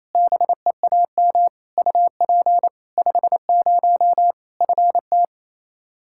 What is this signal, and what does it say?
Morse code, heard as a keyed tone, about 28 wpm, 700 Hz. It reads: BEAM UP 50 FT
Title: K1USN SST QSO Practice 28wpm